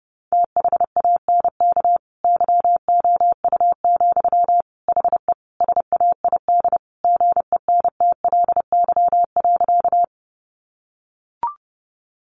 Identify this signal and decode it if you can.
Morse code, heard as a keyed tone, about 30 wpm, 700 Hz. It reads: THANK YOU, HE SAID GENTLY.
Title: Sentences from Top 1500 Words 30wpm